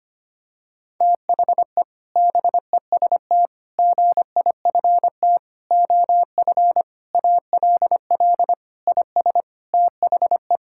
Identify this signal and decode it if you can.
Morse code, heard as a keyed tone, about 25 wpm, 700 Hz. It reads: THE BEST GIFT OF ALL IS THE
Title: Sentences from Top 1000 Words - Review Part 6 - 25wpm